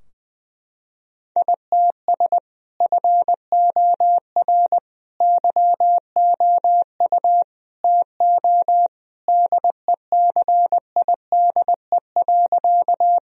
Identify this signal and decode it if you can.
Morse code, heard as a keyed tone, about 21 wpm, 700 Hz. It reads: ITS FOR YOU TO DECIDE.
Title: Sentences from Top 1000 Words - Review Part 3 - 20wpm